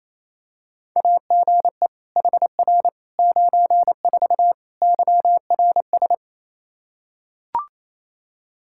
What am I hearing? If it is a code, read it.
Morse code, heard as a keyed tone, about 27 wpm, 700 Hz. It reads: AGE HR 94 YRS
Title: K1USN SST QSO Practice 28wpm